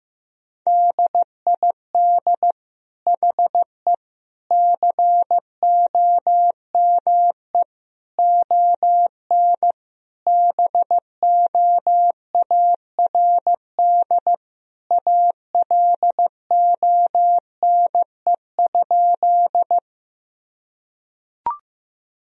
Morse code, heard as a keyed tone, about 15 wpm, 700 Hz. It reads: DID HE COME ON BOARD ALONE?
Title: Sentences from Top 1000 Words - Review Part 4 - 15wpm